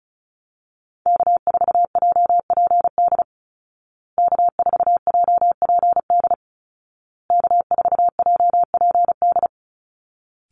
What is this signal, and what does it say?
Morse code, heard as a keyed tone, about 34 wpm, 700 Hz. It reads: K4JPD K4JPD K4JPD
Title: Call Signs Repeated 2x - 35wpm